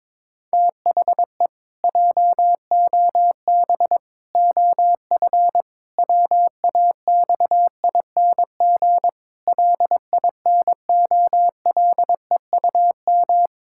Morse code, heard as a keyed tone, about 22 wpm, 700 Hz. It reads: THE JOB OF WAXING LINOLEUM
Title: Pangram Challenge v2 - 22wpm